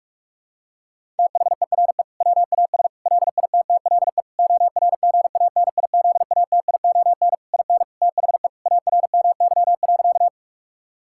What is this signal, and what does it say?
Morse code, heard as a keyed tone, about 47 wpm, 700 Hz. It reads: THERE WAS LITTLE ORGANIZATION IN THE ARMY.
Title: Sentences from Top 1000 Words - Review Part 5 - 45wpm